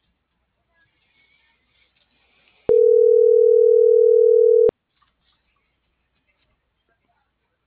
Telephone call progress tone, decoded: ringback tone

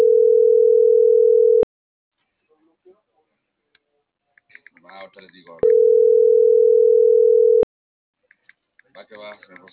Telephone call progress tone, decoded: ringback tone

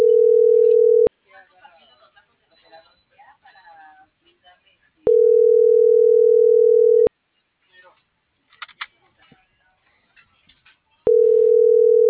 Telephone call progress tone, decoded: ringback tone

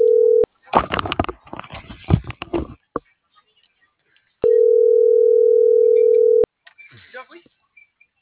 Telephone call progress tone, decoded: ringback tone

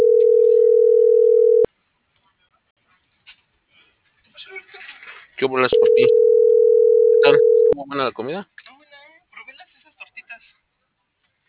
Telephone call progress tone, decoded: ringback tone

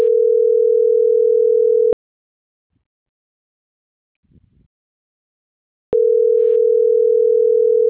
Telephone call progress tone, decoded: ringback tone

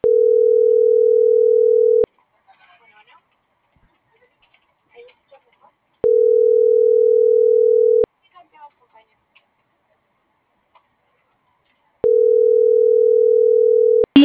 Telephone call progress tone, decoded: ringback tone

